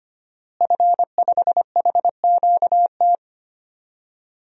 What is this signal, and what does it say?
Morse code, heard as a keyed tone, about 25 wpm, 700 Hz. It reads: F5HQT
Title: DXCC-100 Call Signs - 5 Characters - Repeated 2x - 25wpm